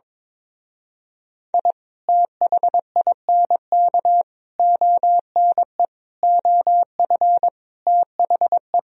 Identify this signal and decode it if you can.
Morse code, heard as a keyed tone, about 22 wpm, 700 Hz. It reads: I THINK ONE OF THE
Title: Speed-Racing - Sets of 5 Words 22wpm